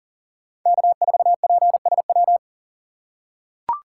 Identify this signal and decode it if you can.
Morse code, heard as a keyed tone, about 38 wpm, 700 Hz. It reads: K4PSW
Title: USA Call Signs - 5 Characters - 3x Character Spacing - 40wpm